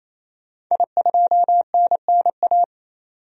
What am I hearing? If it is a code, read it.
Morse code, heard as a keyed tone, about 28 wpm, 700 Hz. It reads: I2NNA
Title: DXCC-100 Call Signs - 5 Characters - 28wpm